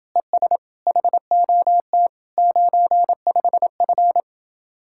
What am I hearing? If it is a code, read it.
Morse code, heard as a keyed tone, about 28 wpm, 700 Hz. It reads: ES HOT 95F